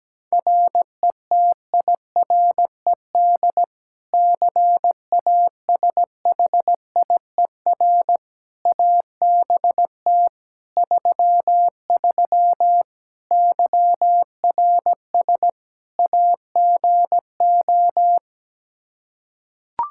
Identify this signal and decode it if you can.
Morse code, heard as a keyed tone, about 17 wpm, 700 Hz. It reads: RETIRED CASHIER ABT 33 YRS AGO
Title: Practice QSO - Exchange 3 - More Info - 17wpm